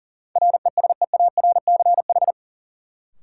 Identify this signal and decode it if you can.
Morse code, heard as a keyed tone, about 40 wpm, 700 Hz. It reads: RESEARCH
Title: Sentences from Top 1000 Words - Review Part 4 - New Word Drill - 40wpm